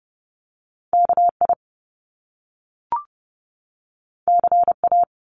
Morse code, heard as a keyed tone, about 30 wpm, 700 Hz. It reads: KI CA
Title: Top Two Letter Combinations in Words 30wpm